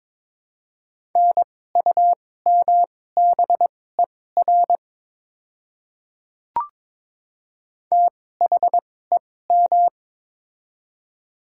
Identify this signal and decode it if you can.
Morse code, heard as a keyed tone, about 22 wpm, spaced as for 16 wpm, 700 Hz. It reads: NUMBER THEM
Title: 2x Character Spacing - Top 100 Words - 22wpm